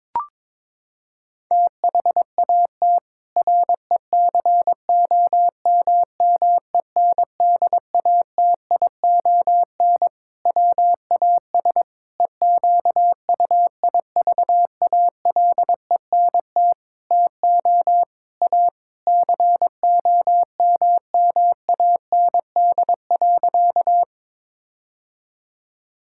Morse code, heard as a keyed tone, about 21 wpm, 700 Hz. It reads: THAT RECOMMENDATION WAS EQUIVALENT TO A COMMAND.
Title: Sentences from Top 1600 Words 22wpm